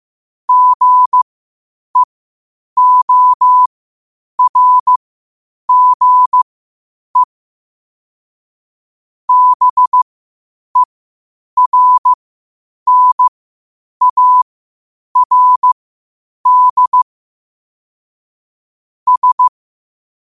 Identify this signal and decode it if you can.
Morse code, heard as a keyed tone, about 15 wpm, spaced as for 8 wpm, 1000 Hz. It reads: GEORGE BERNARD S
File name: qotdfiles / 8wpm